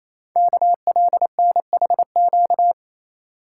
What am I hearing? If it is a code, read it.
Morse code, heard as a keyed tone, about 27 wpm, 700 Hz. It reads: KLNHQ
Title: Code Groups - 5 Letters - 28wpm